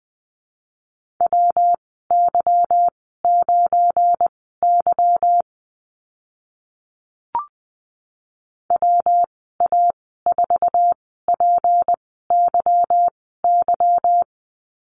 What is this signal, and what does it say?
Morse code, heard as a keyed tone, about 20 wpm, spaced as for 15 wpm, 700 Hz. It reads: WY9Y WA4PYY
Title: 186 Lesson 30g - Letter b - Call Signs 20wpm